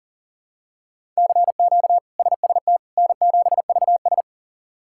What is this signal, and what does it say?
Morse code, heard as a keyed tone, about 40 wpm, 700 Hz. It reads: CQ SST N7VS
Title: K1USN SST QSO Practice - 40wpm